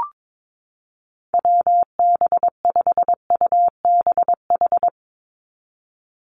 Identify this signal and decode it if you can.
Morse code, heard as a keyed tone, about 22 wpm, 700 Hz. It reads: WB5UBH
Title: USA Call Signs 22wpm